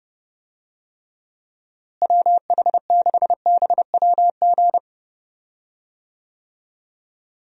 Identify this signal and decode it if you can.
Morse code, heard as a keyed tone, about 30 wpm, 700 Hz. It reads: WH6BWG
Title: USA Call Signs - 6 Characters - Repeated 3x - 30wpm